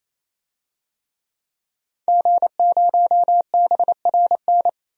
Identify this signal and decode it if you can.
Morse code, heard as a keyed tone, about 28 wpm, 700 Hz. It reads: G0BRN
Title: DXCC-100 Call Signs - 5 Characters - 28wpm